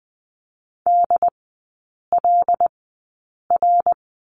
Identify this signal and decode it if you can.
Morse code, heard as a keyed tone, about 20 wpm, spaced as for 8 wpm, 700 Hz. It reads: DLR
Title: 074 Lesson 13e - Letter L - Three Characters 20wpm